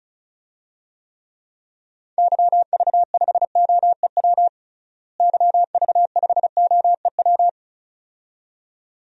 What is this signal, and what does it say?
Morse code, heard as a keyed tone, about 36 wpm, 700 Hz. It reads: YV5OEW YV5OEW
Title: DXCC-100 Call Signs - 6 Characters - Repeated 2x - 35wpm